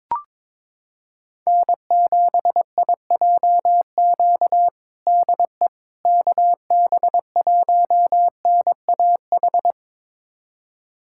Morse code, heard as a keyed tone, about 22 wpm, 700 Hz. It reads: N7IJQ DE KB1NAH
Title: USA Callsign DE Callsign 22wpm